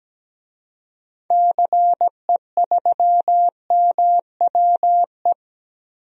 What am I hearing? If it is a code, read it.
Morse code, heard as a keyed tone, about 17 wpm, 700 Hz. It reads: CE3MWE
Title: DXCC-100 Call Signs - 6 Characters - 17wpm